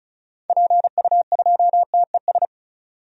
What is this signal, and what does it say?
Morse code, heard as a keyed tone, about 36 wpm, 690 Hz. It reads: PU2TES
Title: DXCC-100 Call Signs - 6 Characters - 35wpm